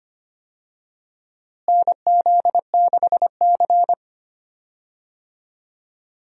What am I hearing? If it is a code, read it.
Morse code, heard as a keyed tone, about 25 wpm, 700 Hz. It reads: NZ6C
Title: USA Call Signs - 4 Characters - Repeated 2x - 25wpm